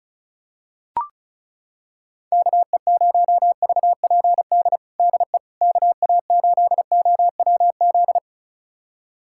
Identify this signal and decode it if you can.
Morse code, heard as a keyed tone, about 34 wpm, 700 Hz. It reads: KE0VPD DE KA8OWZ
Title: USA Callsign DE Callsign - 6 Characters - 35wpm